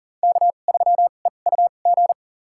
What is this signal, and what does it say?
Morse code, heard as a keyed tone, about 40 wpm, spaced as for 29 wpm, 700 Hz. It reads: K3EUG
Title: USA Call Signs - 2x Character Spacing - 40wpm